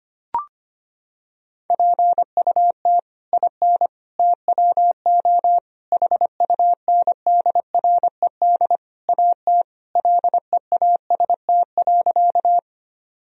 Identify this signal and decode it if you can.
Morse code, heard as a keyed tone, about 25 wpm, 700 Hz. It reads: PUT IN TWO HUNDRED AT LEAST.
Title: Sentences from Top 1000 Words - Review Part 2 - 25wpm